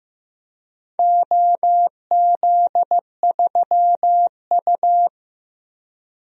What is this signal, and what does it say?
Morse code, heard as a keyed tone, about 15 wpm, 700 Hz. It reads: OZ3U
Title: DXCC-100 Call Signs - 4 Characters - Repeated 3x - 15wpm